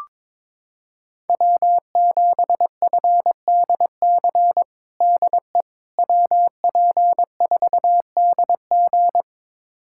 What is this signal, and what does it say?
Morse code, heard as a keyed tone, about 22 wpm, 700 Hz. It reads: W7FDC DE WP4DG
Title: USA Callsign DE Callsign - 5 Characters - 22wpm